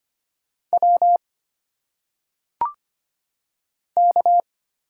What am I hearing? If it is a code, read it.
Morse code, heard as a keyed tone, about 25 wpm, 700 Hz. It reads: W K
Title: Single Letters - Phonetic Pronunciation 25wpm